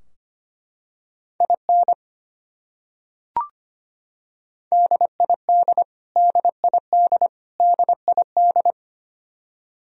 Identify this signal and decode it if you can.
Morse code, heard as a keyed tone, about 25 wpm, 700 Hz. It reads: IN DID DID DID
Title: Top 100 Words Repeated 25wpm